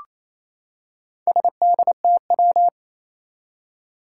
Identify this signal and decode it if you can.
Morse code, heard as a keyed tone, about 28 wpm, 700 Hz. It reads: SDTW
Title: Four Challenging Letter-Number Combinations 28wpm